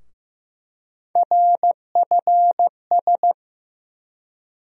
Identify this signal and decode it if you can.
Morse code, heard as a keyed tone, about 15 wpm, 700 Hz. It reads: RFS
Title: USA Call Sign Postfixes - 3 Letters 15wpm